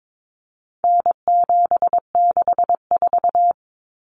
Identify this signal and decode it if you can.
Morse code, heard as a keyed tone, about 22 wpm, 700 Hz. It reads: N764
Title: Common Cut Numbers - 4-Digit 22wpm